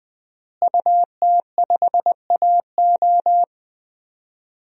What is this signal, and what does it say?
Morse code, heard as a keyed tone, about 21 wpm, 700 Hz. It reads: UT5AO
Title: DXCC-100 Call Signs - 5 Characters - Repeated 3x - 20wpm